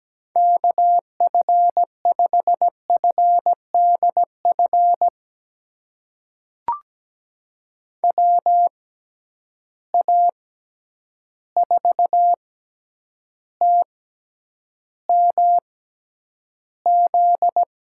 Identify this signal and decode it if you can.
Morse code, heard as a keyed tone, about 17 wpm, spaced as for 9 wpm, 700 Hz. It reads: KF5FDF WA4TMZ
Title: USA Call Signs - 6 Characters - 6x Character Spacing - 17wpm